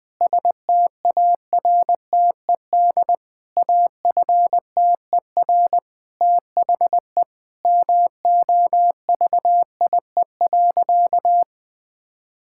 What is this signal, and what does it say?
Morse code, heard as a keyed tone, about 20 wpm, 700 Hz. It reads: STARTED AFTER THE MOVIE.
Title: Sentences with 2-Word Phrases - 20wpm